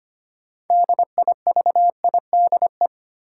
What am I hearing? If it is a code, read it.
Morse code, heard as a keyed tone, about 25 wpm, 700 Hz. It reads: DIVIDE
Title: Sentences from Top 1600 Words - New Word Drill Alt - 25wpm